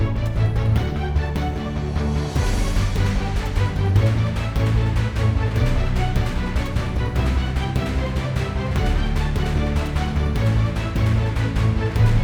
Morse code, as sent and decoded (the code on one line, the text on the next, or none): none